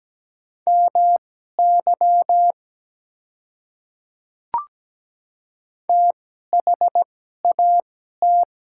Morse code, heard as a keyed tone, about 17 wpm, spaced as for 12 wpm, 700 Hz. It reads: MY THAT
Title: 2x Character Spacing - Top 200 Words - 17wpm